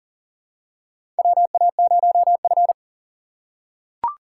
Morse code, heard as a keyed tone, about 37 wpm, 700 Hz. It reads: WA0F
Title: USA Call Signs - 4 Characters - 4x Character Spacing - 40wpm